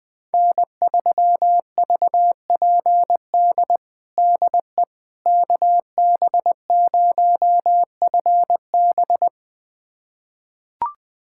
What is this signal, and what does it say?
Morse code, heard as a keyed tone, about 20 wpm, 700 Hz. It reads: N3VPD DE KB0FB
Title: USA Callsign DE Callsign - 5 Characters - 20wpm